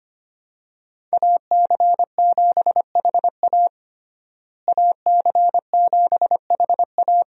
Morse code, heard as a keyed tone, about 25 wpm, 700 Hz. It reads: AC7HA AC7HA
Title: USA Call Signs - 5 Characters - Repeated 2x - 25wpm